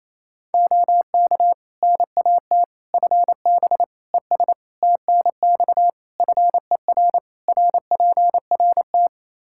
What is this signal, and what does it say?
Morse code, heard as a keyed tone, about 28 wpm, 700 Hz. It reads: OK NAT FB ES TNX FER RPRT
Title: Practice QSO - Exchange 2 - Info - 28wpm